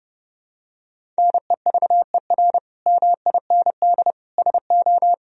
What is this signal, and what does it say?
Morse code, heard as a keyed tone, about 30 wpm, 700 Hz. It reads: NEVER MIND SO
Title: Sentences from Top 1000 Words - Review Part 1 - 30wpm